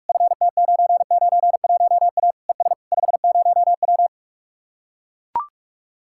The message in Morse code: FT991A ES 50W

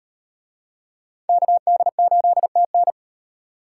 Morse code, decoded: KD8TN

